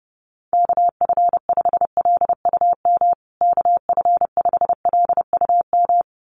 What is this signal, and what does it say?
Morse code, heard as a keyed tone, about 30 wpm, 700 Hz. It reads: KF5LUM KF5LUM
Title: USA Call Signs Repeated 2x 30wpm